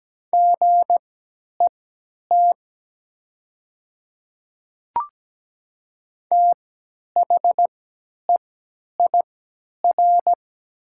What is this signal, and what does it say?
Morse code, heard as a keyed tone, about 17 wpm, spaced as for 10 wpm, 700 Hz. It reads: GET THEIR
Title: 3x Character Spacing - Top 200 Words - 17wpm